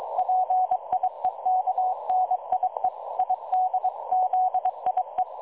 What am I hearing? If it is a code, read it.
Morse code, heard as a keyed tone, about 22 wpm, 740 Hz. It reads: PIEKNIE IDZIE
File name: dlazaawansowanych / nagraniazpasma / websdr recording 3565.0kHz